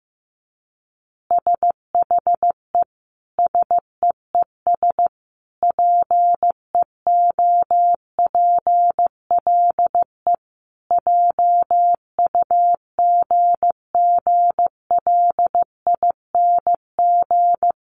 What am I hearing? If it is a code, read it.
Morse code, heard as a keyed tone, about 15 wpm, 700 Hz. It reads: SHE SEES PEOPLE JUGGLING
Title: Easy Phrases 15wpm